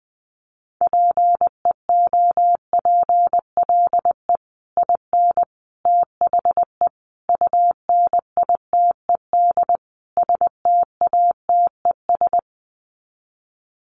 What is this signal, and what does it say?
Morse code, heard as a keyed tone, about 20 wpm, 700 Hz. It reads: PEOPLE IN THE UNITED STATES
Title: Sets of 5 Words Encore 20wpm